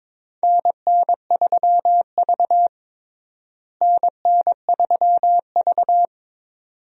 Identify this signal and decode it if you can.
Morse code, heard as a keyed tone, about 22 wpm, 700 Hz. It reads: NN3V NN3V
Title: USA Call Signs - 4 Characters - Repeated 2x - 22wpm